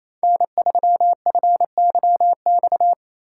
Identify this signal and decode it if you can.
Morse code, heard as a keyed tone, about 34 wpm, 700 Hz. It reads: N3FYX
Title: Speed Racing - USA 1x3 Call Signs 28wpm